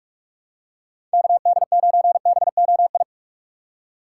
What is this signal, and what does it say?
Morse code, heard as a keyed tone, about 44 wpm, 700 Hz. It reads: KD9BOI
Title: USA Call Signs - 6x Character Spacing - 45wpm